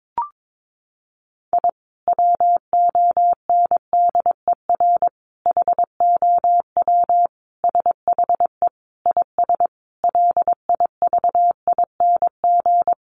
Morse code, heard as a keyed tone, about 22 wpm, 700 Hz. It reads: I WONDER HOW SHE IS LIVING
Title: Sentences from Top 1500 Words 22wpm